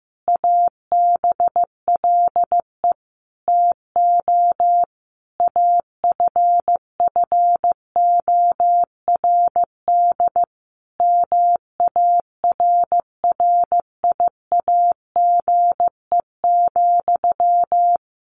Morse code, heard as a keyed tone, about 15 wpm, 700 Hz. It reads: ABLE TO AFFORD MARRIAGE,
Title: Sentences from Top 1100 Words 15wpm